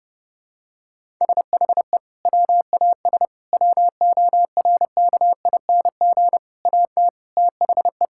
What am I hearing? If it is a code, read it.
Morse code, heard as a keyed tone, about 32 wpm, 700 Hz. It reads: SHE WAS WORKING AT THE
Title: Sentences from Top 1000 Words - Review Part 5 - 30wpm